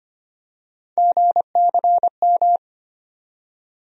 Morse code, heard as a keyed tone, about 25 wpm, 700 Hz. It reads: GCM
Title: USA Call Sign Postfixes - 3 Letters 25wpm